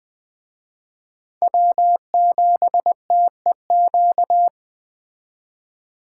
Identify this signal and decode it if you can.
Morse code, heard as a keyed tone, about 21 wpm, 700 Hz. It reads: W7TEQ
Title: Speed Racing - USA 1x3 Call Signs 17wpm